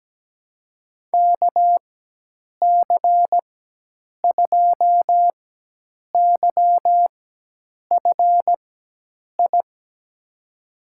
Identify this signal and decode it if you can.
Morse code, heard as a keyed tone, about 17 wpm, spaced as for 8 wpm, 700 Hz. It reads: KC2YFI
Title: USA Call Signs - 4x Character Spacing - 17wpm